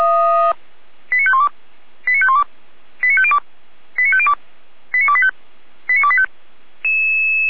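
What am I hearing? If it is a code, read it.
Five-tone selective call (ZVEI), heard as quick runs of five tones, tones of about 70 ms. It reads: 89512 89512 89502 89502 89285 89285
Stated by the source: ZVEI five-tone selective call received by SDR.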